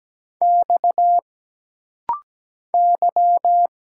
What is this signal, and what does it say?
Morse code, heard as a keyed tone, about 17 wpm, 700 Hz. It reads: X Y
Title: Single Letters - Rapid-Fire 17wpm